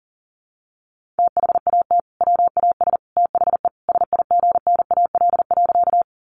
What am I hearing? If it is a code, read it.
Morse code, heard as a keyed tone, about 40 wpm, 700 Hz. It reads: THAT WAS THE SIGNAL.
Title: Sentences from Top 1400 Words 40wpm